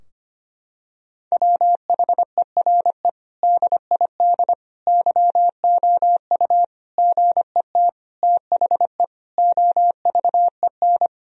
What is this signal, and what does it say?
Morse code, heard as a keyed tone, about 25 wpm, 700 Hz. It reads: WHERE DID YOU GET THE OVEN
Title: Sentences from Top 1000 Words - Review Part 4 - 25wpm